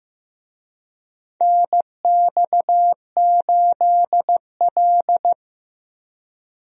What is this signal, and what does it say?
Morse code, heard as a keyed tone, about 15 wpm, 700 Hz. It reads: NX8L
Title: USA Call Signs - 4 Characters - Repeated 2x - 15wpm